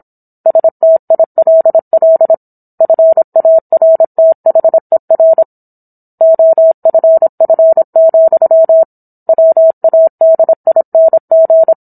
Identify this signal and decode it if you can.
Morse code, heard as a keyed tone, about 27 wpm, 650 Hz. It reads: STILL FARTHER OFF, WADING